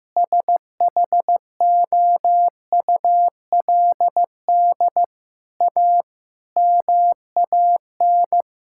Morse code, heard as a keyed tone, about 15 wpm, 700 Hz. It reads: SHOULD A MAN
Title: Sentences from Top 1600 Words 15wpm